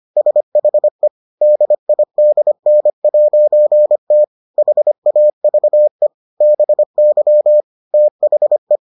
Morse code, heard as a keyed tone, about 25 wpm, 600 Hz. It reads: SHE DIDN'T HAVE BY THE